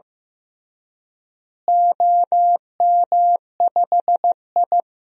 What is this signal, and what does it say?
Morse code, heard as a keyed tone, about 16 wpm, 700 Hz. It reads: OM5I
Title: DXCC-100 Call Signs - 4 Characters - Repeated 3x - 15wpm